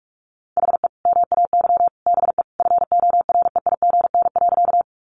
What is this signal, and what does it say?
Morse code, heard as a keyed tone, about 47 wpm, 700 Hz. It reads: HE MAY BE FOREIGN.
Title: Sentences from Top 1400 Words 45wpm